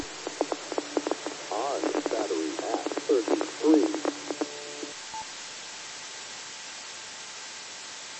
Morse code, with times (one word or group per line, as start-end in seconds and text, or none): none